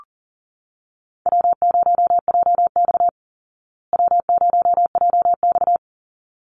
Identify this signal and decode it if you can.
Morse code, heard as a keyed tone, about 40 wpm, 700 Hz. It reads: W0JX W0JX
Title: Easy Call Signs Repeated 2x - 40wpm